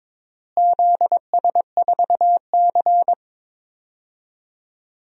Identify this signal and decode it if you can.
Morse code, heard as a keyed tone, about 22 wpm, 700 Hz. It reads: ZS4C
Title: DXCC-100 Call Signs - 4 Characters - Repeated 3x - 22wpm